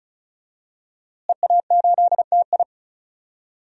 Morse code, heard as a keyed tone, about 34 wpm, 700 Hz. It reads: EA8TI
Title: DXCC-100 Call Signs - 5 Characters - Repeated 2x - 35wpm